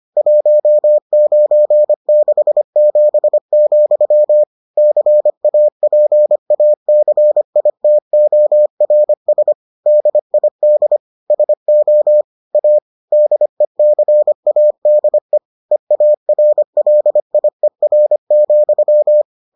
Morse code, heard as a keyed tone, about 25 wpm, 600 Hz. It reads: 1967, CAPACITORS DID SO A DECADE EARLIER,